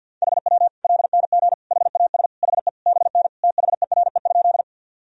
Morse code, heard as a keyed tone, about 52 wpm, 700 Hz. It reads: HW LNG HAS HE BN THERE?
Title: Sentences from Top 100 Words and Abbreviations 50wpm